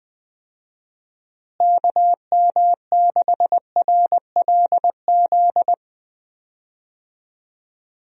Morse code, heard as a keyed tone, about 20 wpm, 700 Hz. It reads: KM6RLZ
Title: USA Call Signs - 6 Characters - Repeated 3x - 20wpm